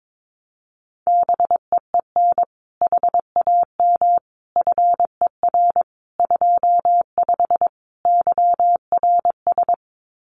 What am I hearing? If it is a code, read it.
Morse code, heard as a keyed tone, about 22 wpm, 700 Hz. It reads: BEEN HAM FER 25 YRS
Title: Common QSO Phrases Encore 22wpm